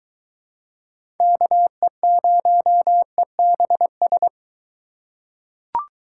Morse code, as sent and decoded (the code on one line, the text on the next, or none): -.- . ----- . -... ...
KE0EBS